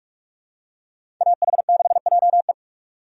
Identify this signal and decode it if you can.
Morse code, heard as a keyed tone, about 41 wpm, 700 Hz. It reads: AH6JE
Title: USA Call Signs - 5 Characters - 3x Character Spacing - 45wpm